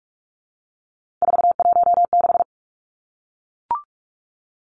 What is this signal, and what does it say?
Morse code, heard as a keyed tone, about 44 wpm, 700 Hz. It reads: 416
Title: Numbers - 3-Digit 45wpm